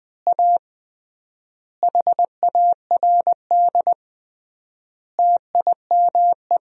Morse code, heard as a keyed tone, about 21 wpm, 700 Hz. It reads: A HARD TIME
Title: 3x Word-Spacing - Sentences from Top 300 Words - 20wpm